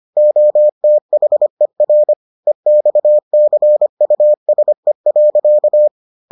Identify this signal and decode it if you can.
Morse code, heard as a keyed tone, about 25 wpm, 600 Hz. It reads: OTHER EXCUSE.